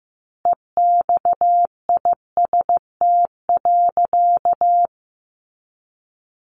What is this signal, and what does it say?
Morse code, heard as a keyed tone, about 15 wpm, 700 Hz. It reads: EXIST.
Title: A Sound Of Thunder 15wpm